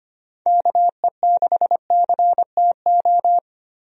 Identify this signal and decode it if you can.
Morse code, heard as a keyed tone, about 25 wpm, 700 Hz. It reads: KE6CTO
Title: USA Call Signs - 6 Characters - Repeated 2x - 25wpm